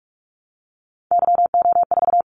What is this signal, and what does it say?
Morse code, heard as a keyed tone, about 46 wpm, 700 Hz. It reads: YO4
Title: Callsign Prefixes for Top 25 DXCC - 45wpm